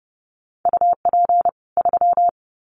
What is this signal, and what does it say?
Morse code, heard as a keyed tone, about 30 wpm, 700 Hz. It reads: UP 3
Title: Common QSO Phrases 30wpm